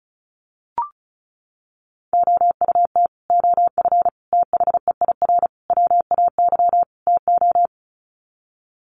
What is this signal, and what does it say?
Morse code, heard as a keyed tone, about 34 wpm, 700 Hz. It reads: OUT OF THEIR WAY TO
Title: Sets of 5 Words Encore2 35wpm